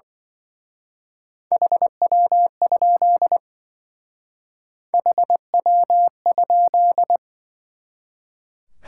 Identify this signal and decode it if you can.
Morse code, heard as a keyed tone, about 24 wpm, 700 Hz. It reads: HW? HW?
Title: Speed-Racing Top 180 QSO Elements 20wpm